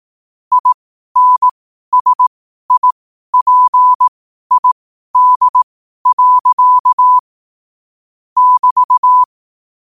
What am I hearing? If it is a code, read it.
Morse code, heard as a keyed tone, about 18 wpm, spaced as for 12 wpm, 1000 Hz. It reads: INSIPID. <BT>
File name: qotdfiles / 12wpm